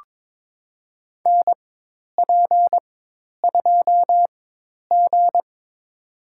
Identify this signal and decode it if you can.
Morse code, heard as a keyed tone, about 21 wpm, spaced as for 10 wpm, 700 Hz. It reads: NP2G
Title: USA Call Signs - 4 Characters - 4x Character Spacing - 22wpm